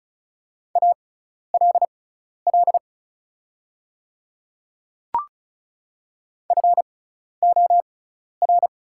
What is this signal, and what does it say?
Morse code, heard as a keyed tone, about 36 wpm, spaced as for 12 wpm, 700 Hz. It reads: ALL FOR
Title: 6x Character Spacing - Top 100 Words - 35wpm